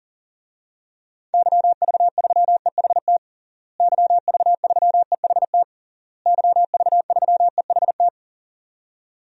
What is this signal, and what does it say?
Morse code, heard as a keyed tone, about 40 wpm, 700 Hz. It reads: YV3EHT YV3EHT YV3EHT
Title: DXCC-100 Call Signs - 6 Characters - Repeated 3x - 40wpm